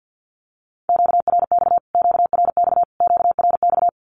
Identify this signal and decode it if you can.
Morse code, heard as a keyed tone, about 50 wpm, 700 Hz. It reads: QRX QRX QRX
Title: Q-Codes Repeated 3x 50wpm